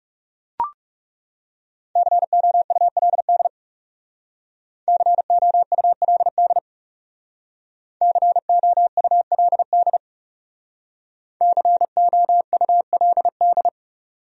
Morse code, heard as a keyed tone, about 37 wpm, 700 Hz. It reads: COULD COULD COULD COULD
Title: Speed-Racing Top 100 Words 30wpm